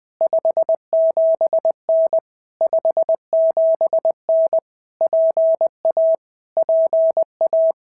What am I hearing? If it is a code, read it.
Morse code, heard as a keyed tone, about 20 wpm, 650 Hz. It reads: 57N 57N PA PA
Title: POTA QSO Practice - 20wpm